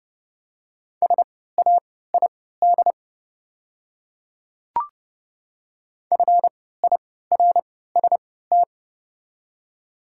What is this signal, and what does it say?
Morse code, heard as a keyed tone, about 29 wpm, spaced as for 17 wpm, 700 Hz. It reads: SAID FIRST
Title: 3x Character Spacing - Top 100 Words - 30wpm